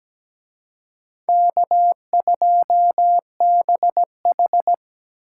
Morse code, heard as a keyed tone, about 20 wpm, 700 Hz. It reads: K2BH
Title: Speed Racing - USA 1x2 Call Signs 17wpm